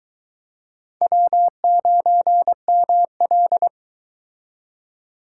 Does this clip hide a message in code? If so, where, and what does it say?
Yes: Morse code from 1.0 to 3.7 s, heard as a keyed tone, about 23 wpm, 700 Hz. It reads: W9ML